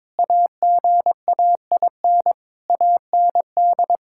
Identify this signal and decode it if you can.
Morse code, heard as a keyed tone, about 22 wpm, 700 Hz. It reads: AGAIN AND
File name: practice / 22wpm / Sentences with 8-Word Phrases - 22wpm